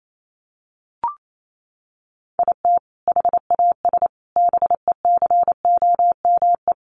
Morse code, heard as a keyed tone, about 28 wpm, 700 Hz. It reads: IT HAS BECOME
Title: Sentences from Top 1200 Words 28wpm